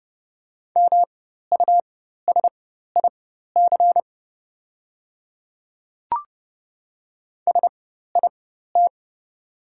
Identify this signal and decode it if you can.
Morse code, heard as a keyed tone, about 30 wpm, spaced as for 14 wpm, 700 Hz. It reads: MUSIC SIT
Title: 4x Character Spacing - Top 500 Words - 30wpm